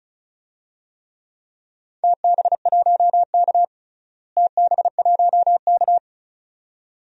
Morse code, heard as a keyed tone, about 34 wpm, 700 Hz. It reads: TB1K TB1K
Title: DXCC-100 Call Signs - 4 Characters - Repeated 2x - 35wpm